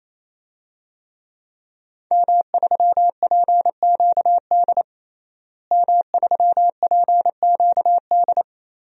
Morse code, heard as a keyed tone, about 27 wpm, 700 Hz. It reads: M3PQD M3PQD
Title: DXCC-100 Call Signs - 5 Characters - Repeated 3x - 28wpm